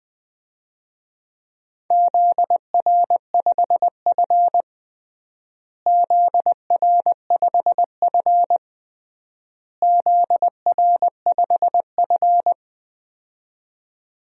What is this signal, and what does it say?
Morse code, heard as a keyed tone, about 20 wpm, 700 Hz. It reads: ZR5F ZR5F ZR5F
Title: DXCC-100 Call Signs - 4 Characters - Repeated 3x - 20wpm